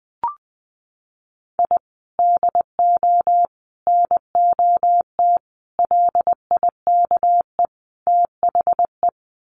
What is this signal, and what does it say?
Morse code, heard as a keyed tone, about 20 wpm, 700 Hz. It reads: I DO NOT LIKE THE
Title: Sets of 5 Words Encore 20wpm